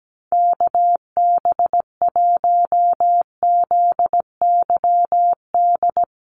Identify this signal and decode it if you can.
Morse code, heard as a keyed tone, about 17 wpm, 700 Hz. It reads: KB1ZYD
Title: USA Call Signs Repeated 3x 17wpm